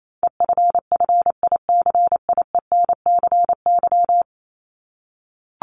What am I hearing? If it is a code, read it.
Morse code, heard as a keyed tone, about 28 wpm, 700 Hz. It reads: EFFICIENCY
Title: Top 1501-1750 Words 28wpm